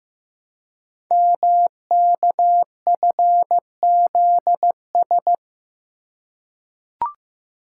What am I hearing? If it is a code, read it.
Morse code, heard as a keyed tone, about 15 wpm, 700 Hz. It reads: MKFZS
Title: Code Groups - 5 Letters - 15wpm